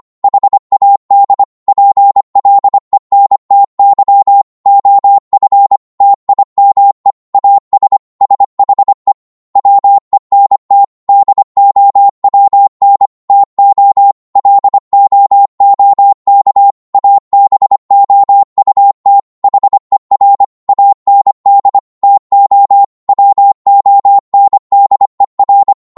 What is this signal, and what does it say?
Morse code, heard as a keyed tone, about 25 wpm, 800 Hz. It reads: HAD PLENTY OF TIME AS SHE WENT DOWN TO LOOK ABOUT HER AND TO WONDER